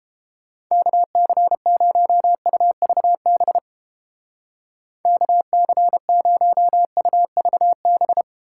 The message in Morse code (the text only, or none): KC0UVB KC0UVB